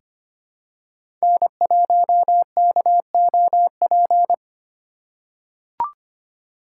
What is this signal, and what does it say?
Morse code, heard as a keyed tone, about 25 wpm, 700 Hz. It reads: N1KOP
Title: USA Call Signs - 5 Characters - 2x Character Spacing - 25wpm